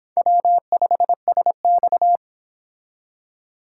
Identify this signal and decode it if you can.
Morse code, heard as a keyed tone, about 26 wpm, 700 Hz. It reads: W5SX